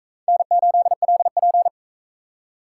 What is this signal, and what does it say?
Morse code, heard as a keyed tone, about 40 wpm, 700 Hz. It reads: N8LP